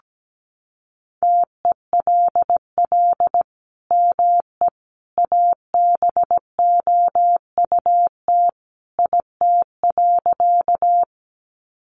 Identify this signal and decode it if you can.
Morse code, heard as a keyed tone, about 17 wpm, 700 Hz. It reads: TELL ME ABOUT IT.
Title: Sentences from Top 200 Words 17wpm